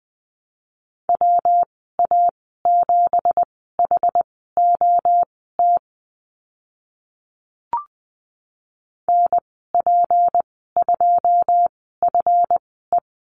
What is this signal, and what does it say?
Morse code, heard as a keyed tone, about 20 wpm, spaced as for 14 wpm, 700 Hz. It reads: WA7HOT NP2FE
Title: 167 Lesson 27g - Number 7 - Call Signs 20wpm